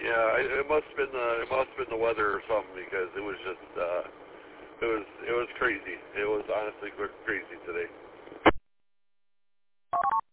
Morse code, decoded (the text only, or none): none